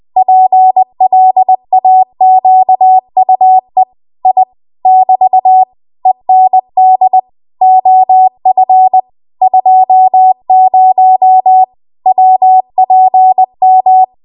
Morse code, heard as a keyed tone, about 20 wpm, 750 Hz. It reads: PLAQUE I <BT> END OF 20 WPM
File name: Archive / 20 WPM / 20WPM